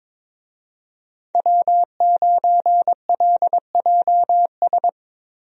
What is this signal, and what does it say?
Morse code, heard as a keyed tone, about 21 wpm, 700 Hz. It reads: W9LJS
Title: USA Call Signs - 5 Characters - Repeated 3x - 22wpm